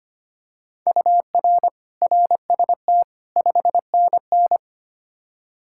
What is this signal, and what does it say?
Morse code, heard as a keyed tone, about 25 wpm, 700 Hz. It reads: UR RST 5NN
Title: K1USN SST QSO Practice 25wpm